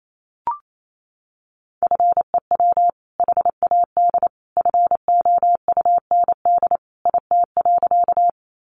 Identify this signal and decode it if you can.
Morse code, heard as a keyed tone, about 28 wpm, 700 Hz. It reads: FEW HAD FOUND IT.
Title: Sentences from Top 300 Words 28wpm